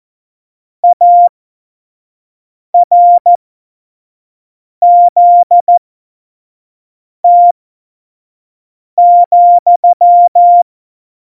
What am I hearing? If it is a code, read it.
Morse code, heard as a keyed tone, about 14 wpm, spaced as for 5 wpm, 700 Hz. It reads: ARZT,
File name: Morsekurs / Witze / Witz 21 Juli 2020, 25 F70